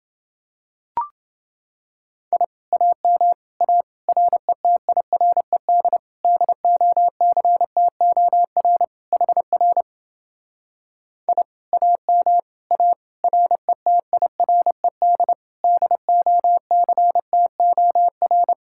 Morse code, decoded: I AM A RETIRED DOCTOR HR I AM A RETIRED DOCTOR

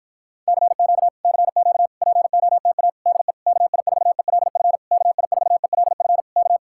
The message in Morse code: CQ CQ POTA DE KI4ELF KI4ELF K

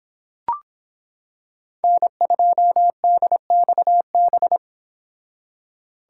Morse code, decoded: N2DXB

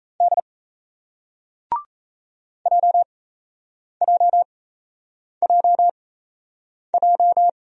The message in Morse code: D JJJJ

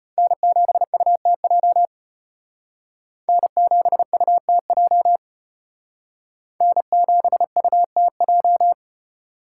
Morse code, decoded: N7UTJ N7UTJ N7UTJ

